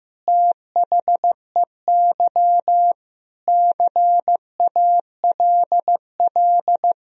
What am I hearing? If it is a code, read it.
Morse code, heard as a keyed tone, about 15 wpm, 700 Hz. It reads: THEY CALL
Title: Sentences from Top 100 Words and Abbreviations 15wpm